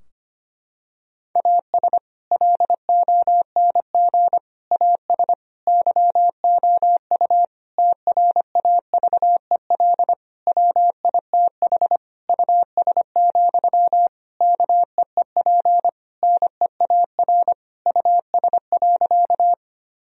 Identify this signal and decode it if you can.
Morse code, heard as a keyed tone, about 25 wpm, 700 Hz. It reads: AS LONG AS YOU TRAVEL WITH US, KEEP NEAR US.
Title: Sentences from Top 1000 Words - Review Part 2 - 25wpm